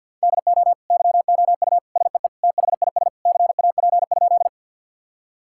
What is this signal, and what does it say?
Morse code, heard as a keyed tone, about 50 wpm, 700 Hz. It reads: DO YOU SEE THIS CAP?
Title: Sentences from Top 1600 Words 50wpm